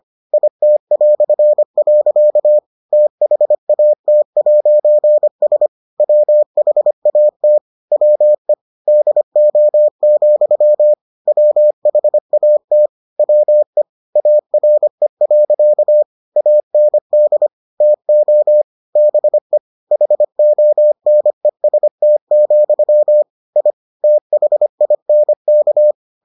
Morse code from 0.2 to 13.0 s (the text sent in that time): IT". THAT'S WHAT WE DO, WHAT